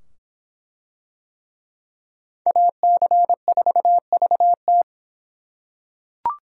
Morse code, heard as a keyed tone, about 24 wpm, 700 Hz. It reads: AC4VT